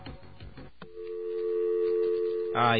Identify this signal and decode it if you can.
Dial tone (North America), 350 + 440 Hz.